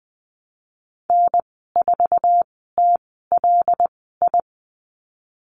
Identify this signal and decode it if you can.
Morse code, heard as a keyed tone, about 20 wpm, spaced as for 15 wpm, 700 Hz. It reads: N4TLI
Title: 088 Lesson 15g - Number 5 - Call Signs 20wpm